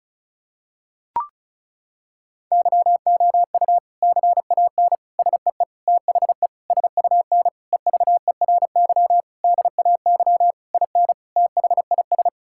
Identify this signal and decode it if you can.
Morse code, heard as a keyed tone, about 34 wpm, 700 Hz. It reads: YOU CAN SEE THE SUN EVERY DAY IN THIS
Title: Sentences from Top 1600 Words 35wpm